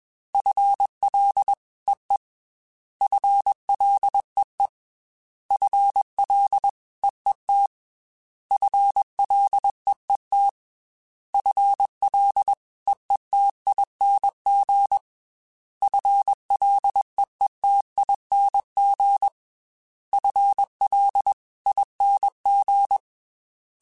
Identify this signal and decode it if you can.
Morse code, heard as a keyed tone, about 21 wpm, 800 Hz. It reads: FLEE FLEE FLEET FLEET FLEETING FLEETING FLING
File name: FL / FL mp3s / FL Prefixes 3x2 sp @20